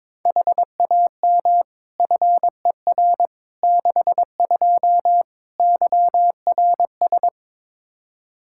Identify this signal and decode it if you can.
Morse code, heard as a keyed tone, about 22 wpm, 700 Hz. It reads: HAM FER 62 YRS
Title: Practice QSO - Exchange 3 - More Info - 22wpm